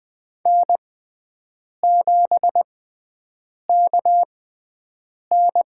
Morse code, heard as a keyed tone, about 20 wpm, spaced as for 7 wpm, 700 Hz. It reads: N7KN
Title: USA Call Signs - 4 Characters - 6x Character Spacing - 20wpm